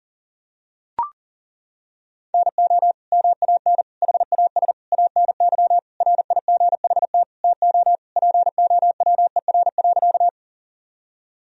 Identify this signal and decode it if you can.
Morse code, heard as a keyed tone, about 40 wpm, 700 Hz. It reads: NO MAN HAS ANY RIGHT TO POWER.
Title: Sentences from Top 1000 Words - Review Part 2 - 40wpm